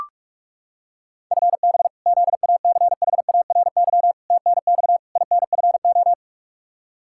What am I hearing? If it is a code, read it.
Morse code, heard as a keyed tone, about 41 wpm, 700 Hz. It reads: FB ZACHARY TNX INFO
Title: Practice QSO - Exchange 3 - More Info - 45wpm